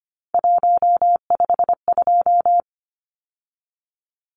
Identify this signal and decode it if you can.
Morse code, heard as a keyed tone, about 25 wpm, 700 Hz. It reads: 152
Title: Numbers - 3-Digit 25wpm